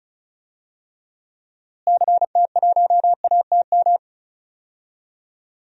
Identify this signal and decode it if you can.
Morse code, heard as a keyed tone, about 36 wpm, 700 Hz. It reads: CT1ATM
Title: DXCC-100 Call Signs - 6 Characters - Repeated 2x - 35wpm